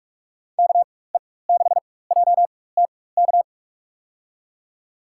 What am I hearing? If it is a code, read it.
Morse code, heard as a keyed tone, about 46 wpm, spaced as for 21 wpm, 700 Hz. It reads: KE6JTK
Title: USA Call Signs - 6 Characters - 4x Character Spacing - 45wpm